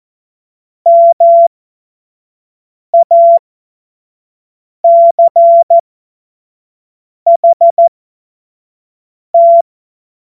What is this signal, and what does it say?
Morse code, heard as a keyed tone, about 14 wpm, spaced as for 5 wpm, 680 Hz. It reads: MACHT
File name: Morsekurs / Witze / Witz 11. Mai 2021, 25 F70